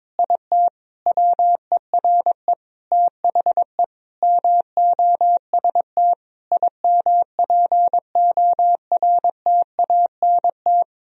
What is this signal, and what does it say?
Morse code, heard as a keyed tone, about 22 wpm, 700 Hz. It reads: IT WERE THE MOST IMPORTANT
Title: Sentences from Top 1000 Words - Review Part 3 - 22wpm